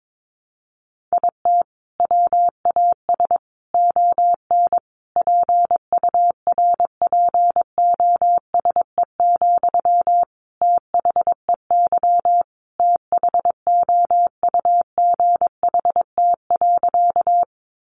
Morse code, heard as a keyed tone, about 22 wpm, 700 Hz. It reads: IT WAS ON PURPOSE, THEY THOUGHT.
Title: Sentences from Top 800 Words 22wpm